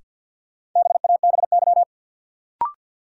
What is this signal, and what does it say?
Morse code, heard as a keyed tone, about 53 wpm, 700 Hz. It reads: BABY
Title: Sentences from Top 1300 Words - New Word Drill Alt - 50wpm